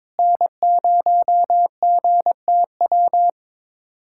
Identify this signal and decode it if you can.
Morse code, heard as a keyed tone, about 22 wpm, 700 Hz. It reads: N0GTW
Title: USA Callsign DE Callsign - 5 Characters - 22wpm